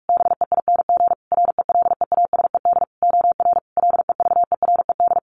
Morse code, heard as a keyed tone, about 44 wpm, 700 Hz. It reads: BEING RELEASED OR LEVERED
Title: A Sound Of Thunder 45wpm